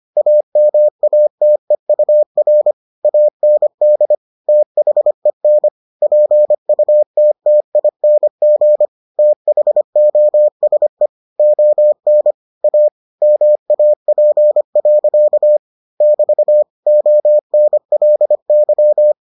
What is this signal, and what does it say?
Morse code, heard as a keyed tone, about 25 wpm, 600 Hz. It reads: AMATEUR AND THEN PUTTING THOSE ON A MAP. <BT> ONLY